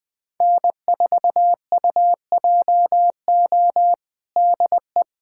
Morse code, heard as a keyed tone, about 21 wpm, 700 Hz. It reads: N4UJO DE
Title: USA Callsign DE Callsign - 5 Characters - 20wpm